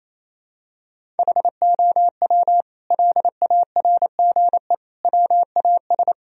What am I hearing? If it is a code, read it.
Morse code, heard as a keyed tone, about 28 wpm, 700 Hz. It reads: HOW LARGE WAS
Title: Sentences from Top 1000 Words - Review Part 1 - 28wpm